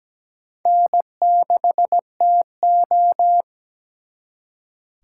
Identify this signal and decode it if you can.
Morse code, heard as a keyed tone, about 20 wpm, 700 Hz. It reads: N6TO
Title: Speed Racing - USA 1x2 Call Signs 17wpm